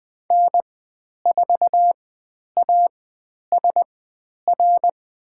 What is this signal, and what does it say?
Morse code, heard as a keyed tone, about 20 wpm, spaced as for 10 wpm, 700 Hz. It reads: N4ASR
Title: 109 Lesson 18g - Letter m - Call Signs 20wpm